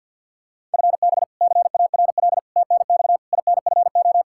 Morse code, heard as a keyed tone, about 47 wpm, 700 Hz. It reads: FB CARL TNX INFO
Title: Practice QSO - Exchange 3 - More Info - 50wpm